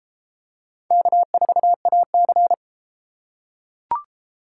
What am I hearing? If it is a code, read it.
Morse code, heard as a keyed tone, about 30 wpm, 700 Hz. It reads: K4AC